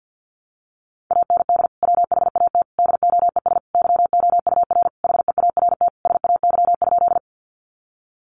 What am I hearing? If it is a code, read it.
Morse code, heard as a keyed tone, about 50 wpm, 700 Hz. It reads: AND WHAT DOES YOUR HEART SAY?
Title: Sentences from Top 1200 Words 50wpm